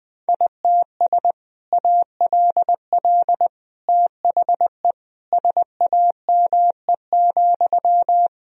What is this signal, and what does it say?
Morse code, heard as a keyed tone, about 20 wpm, 700 Hz. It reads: ITS ALL THE SAME,
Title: Sentences from Top 1000 Words - Review Part 3 - 20wpm